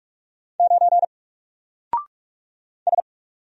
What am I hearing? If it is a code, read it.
Morse code, heard as a keyed tone, about 44 wpm, 700 Hz. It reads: 9 S
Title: Single Letter-Number Alt 45wpm